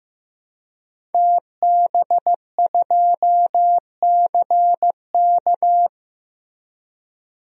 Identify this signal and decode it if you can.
Morse code, heard as a keyed tone, about 15 wpm, 700 Hz. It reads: TB2CK
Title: DXCC-100 Call Signs - 5 Characters - Repeated 2x - 15wpm